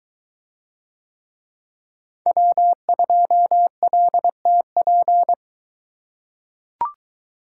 Morse code, heard as a keyed tone, about 23 wpm, 700 Hz. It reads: W2LTP